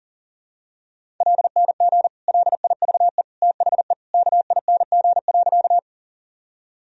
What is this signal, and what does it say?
Morse code, heard as a keyed tone, about 37 wpm, 700 Hz. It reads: LNG LIVE THE KING.
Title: Sentences from Top 400 Words and Abbreviations 40wpm